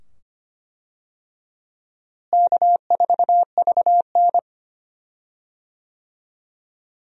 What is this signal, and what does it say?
Morse code, heard as a keyed tone, about 25 wpm, 700 Hz. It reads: K4VN
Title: USA Call Signs - 4 Characters - Repeated 2x - 25wpm